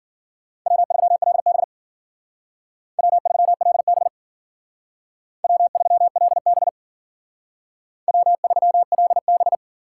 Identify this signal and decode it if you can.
Morse code, heard as a keyed tone, about 52 wpm, 700 Hz. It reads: W3LB W3LB W3LB W3LB